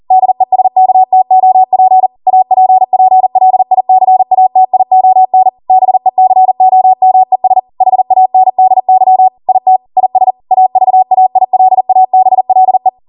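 Morse code, heard as a keyed tone, about 40 wpm, 750 Hz. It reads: DESKTOP APPLICATION BECOMES HANDY IT IS AVAILABLE
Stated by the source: ARRL 40 WPM Morse code practice transmission.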